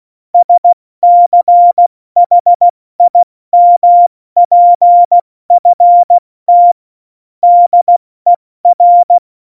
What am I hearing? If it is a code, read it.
Morse code, heard as a keyed tone, about 16 wpm, spaced as for 14 wpm, 700 Hz. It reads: SCHIMPFT DER
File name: Morsekurs / Witze / Witz 09. April 2019, 70 F80